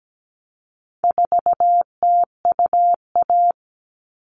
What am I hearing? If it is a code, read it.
Morse code, heard as a keyed tone, about 17 wpm, 700 Hz. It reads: 4TUA
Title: All Cut Numbers - 4-Digit 17wpm